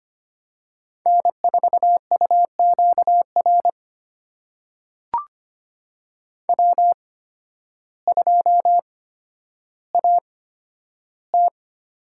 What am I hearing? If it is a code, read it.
Morse code, heard as a keyed tone, about 25 wpm, spaced as for 12 wpm, 700 Hz. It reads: N4UQR W2AT
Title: USA Call Signs - 5 Characters - 8x Character Spacing - 25wpm